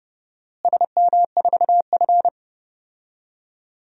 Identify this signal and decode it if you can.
Morse code, heard as a keyed tone, about 30 wpm, 700 Hz. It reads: SM4F
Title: DXCC-100 Call Signs - 4 Characters - Repeated 3x - 30wpm